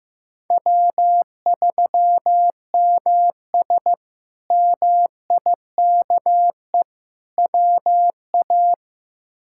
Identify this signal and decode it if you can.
Morse code, heard as a keyed tone, about 15 wpm, 700 Hz. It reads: W3MS MIKE WA
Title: K1USN SST QSO Practice - 15wpm